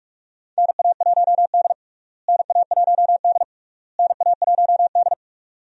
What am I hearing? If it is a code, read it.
Morse code, heard as a keyed tone, about 41 wpm, 700 Hz. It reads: NA1D NA1D NA1D
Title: USA Call Signs - 4 Characters - Repeated 3x - 45wpm